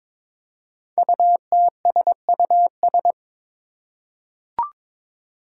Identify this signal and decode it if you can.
Morse code, heard as a keyed tone, about 22 wpm, 700 Hz. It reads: UTSUS
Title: Five Challenging Letter-Number Combinations 22wpm